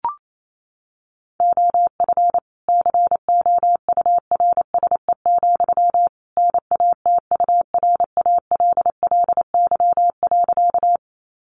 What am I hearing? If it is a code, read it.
Morse code, heard as a keyed tone, about 28 wpm, 700 Hz. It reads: OF COURSE, NATURALLY.
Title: Sentences from Top 1300 Words 28wpm